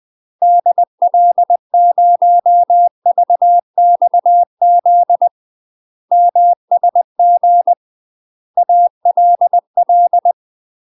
Morse code, heard as a keyed tone, about 20 wpm, 700 Hz. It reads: DL0VXZ MSG ALL
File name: UK MASTER QSOs TRAINING / 20WPM / UK MASTER QSOs 0018